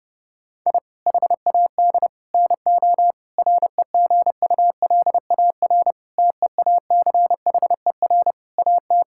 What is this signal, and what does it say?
Morse code, heard as a keyed tone, about 30 wpm, 700 Hz. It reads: I HAD NO REGULAR TEACHER AT
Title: Sentences from Top 1000 Words - Review Part 6 - 30wpm